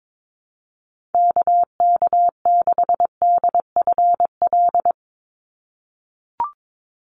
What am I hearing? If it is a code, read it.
Morse code, heard as a keyed tone, about 22 wpm, 700 Hz. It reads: KK6DFL
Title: USA Call Signs 22wpm